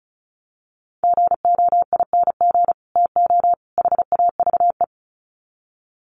Morse code, heard as a keyed tone, about 34 wpm, 700 Hz. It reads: GOING TO HAVE
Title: Sets of 3 Words 35wpm